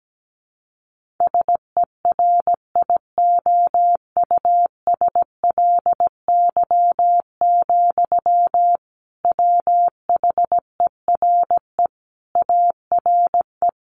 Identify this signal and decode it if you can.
Morse code, heard as a keyed tone, about 17 wpm, 700 Hz. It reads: SERIOUSLY, WHERE ARE
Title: Sentences from Top 1200 Words 17wpm